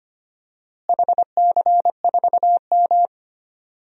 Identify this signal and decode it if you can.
Morse code, heard as a keyed tone, about 25 wpm, 700 Hz. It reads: HC4M
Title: DXCC-100 Call Signs - 4 Characters - Repeated 3x - 25wpm